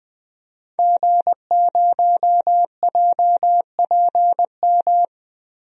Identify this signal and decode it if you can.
Morse code, heard as a keyed tone, about 20 wpm, 700 Hz. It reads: G0JPM
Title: DXCC-100 Call Signs - 5 Characters - 20wpm